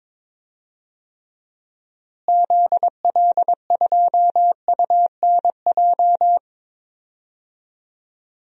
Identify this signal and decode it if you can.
Morse code, heard as a keyed tone, about 22 wpm, 700 Hz. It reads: ZL2UNJ
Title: DXCC-100 Call Signs - 6 Characters - Repeated 3x - 22wpm